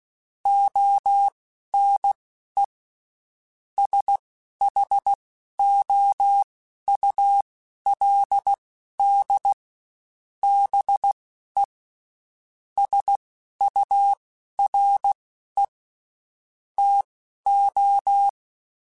Morse code, heard as a keyed tone, about 16 wpm, spaced as for 11 wpm, 800 Hz. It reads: ONE SHOULD BE SURE TO
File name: Spaced / Chuckles Spaced-mp3s / BrainMouth sp @15